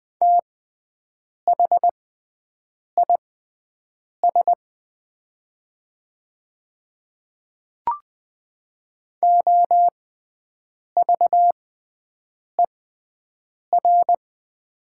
Morse code, heard as a keyed tone, about 20 wpm, spaced as for 7 wpm, 700 Hz. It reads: THIS OVER
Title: 6x Character Spacing - Top 100 Words - 20wpm